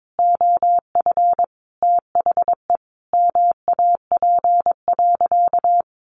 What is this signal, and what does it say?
Morse code, heard as a keyed tone, about 22 wpm, 700 Hz. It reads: OF THE MAP.
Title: Sentences from Top 400 Words 22wpm